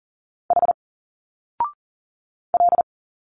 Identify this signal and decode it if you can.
Morse code, heard as a keyed tone, about 39 wpm, 700 Hz. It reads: H L
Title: Single Letters 40wpm